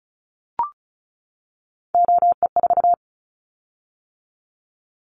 Morse code, heard as a keyed tone, about 36 wpm, 700 Hz. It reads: OE4
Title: Callsign Prefixes for Top 50 DXCC - 35wpm